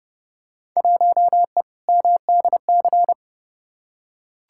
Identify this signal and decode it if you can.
Morse code, heard as a keyed tone, about 30 wpm, 700 Hz. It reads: 1E MDC
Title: Field Day Exchanges Encore - 30wpm